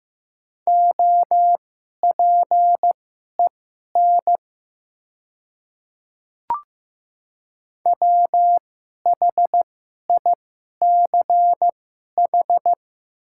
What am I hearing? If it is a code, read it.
Morse code, heard as a keyed tone, about 15 wpm, spaced as for 11 wpm, 700 Hz. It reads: OPEN WHICH
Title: 2x Character Spacing - Top 300 Words - 15wpm